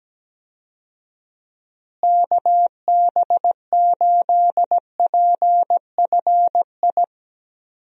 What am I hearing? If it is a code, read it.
Morse code, heard as a keyed tone, about 17 wpm, 700 Hz. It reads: KB8PFI
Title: USA Call Signs - 6 Characters - Repeated 2x - 17wpm